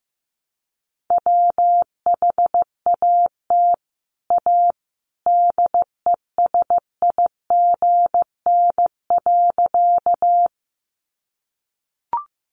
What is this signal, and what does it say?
Morse code, heard as a keyed tone, about 15 wpm, 700 Hz. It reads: WHAT A DESIGN.
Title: Sentences from Top 800 Words 15wpm